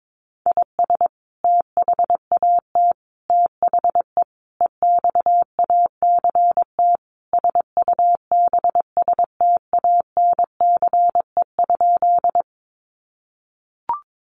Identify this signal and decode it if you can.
Morse code, heard as a keyed tone, about 22 wpm, 700 Hz. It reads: IS THAT THE EXACT SUBSTANCE?
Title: Sentences from Top 1500 Words 22wpm